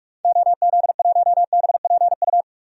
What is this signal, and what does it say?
Morse code, heard as a keyed tone, about 42 wpm, 700 Hz. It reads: OZ1BPU
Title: DXCC-100 Call Signs - 6 Characters - 45wpm